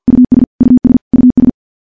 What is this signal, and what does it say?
Morse code, heard as a keyed tone, about 50 wpm, 250 Hz. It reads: US US US
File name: morse / morse-training-50WPM